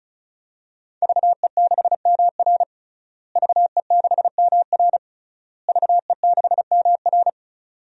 Morse code, heard as a keyed tone, about 36 wpm, 700 Hz. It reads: VE6MR VE6MR VE6MR
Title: DXCC-100 Call Signs - 5 Characters - Repeated 3x - 35wpm